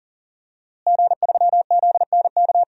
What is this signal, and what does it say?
Morse code, heard as a keyed tone, about 40 wpm, 700 Hz. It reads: G3ZNK
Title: DXCC-100 Call Signs - 5 Characters - Repeated 3x - 40wpm